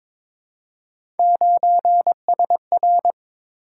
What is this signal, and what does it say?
Morse code, heard as a keyed tone, about 22 wpm, 700 Hz. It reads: 9SR
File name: practice / 22wpm / Three Letter-Number Combinations - 22wpm